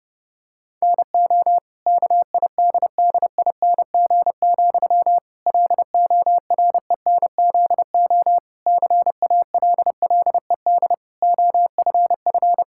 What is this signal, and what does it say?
Morse code, heard as a keyed tone, about 30 wpm, 700 Hz. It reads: NO KIDDING, LORENZO CALLED OFF
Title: Pangram Challenge v2 - 30wpm